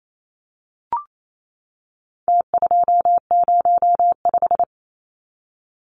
Morse code, heard as a keyed tone, about 28 wpm, 700 Hz. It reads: T205
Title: Numbers - Contest Style - 4-Digit 28wpm